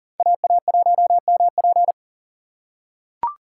Morse code, heard as a keyed tone, about 40 wpm, 700 Hz. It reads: AA1MP
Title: USA Call Signs - 5 Characters - 3x Character Spacing - 40wpm